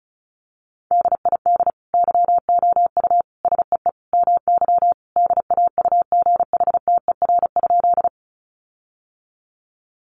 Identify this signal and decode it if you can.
Morse code, heard as a keyed tone, about 34 wpm, 700 Hz. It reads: DID YOU SEE MY DAUGHTER?
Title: Sentences from Top 1500 Words 35wpm